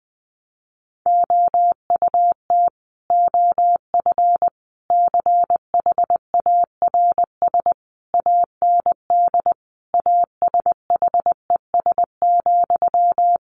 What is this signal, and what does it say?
Morse code, heard as a keyed tone, about 20 wpm, 700 Hz. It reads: OUT OF CHARS AND ASHES,
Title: A Sound Of Thunder 20wpm